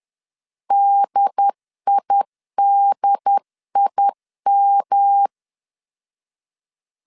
Morse code, heard as a keyed tone, about 10 wpm, 800 Hz. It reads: DIDIM
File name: 10wpm